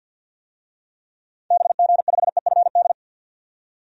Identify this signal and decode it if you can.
Morse code, heard as a keyed tone, about 48 wpm, 700 Hz. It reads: BG5EFD
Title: DXCC-100 Call Signs - 6 Characters - 50wpm